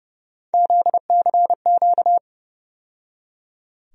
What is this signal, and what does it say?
Morse code, heard as a keyed tone, about 31 wpm, 700 Hz. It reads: ZCQ
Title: USA Call Sign Postfixes - 3 Letters 30wpm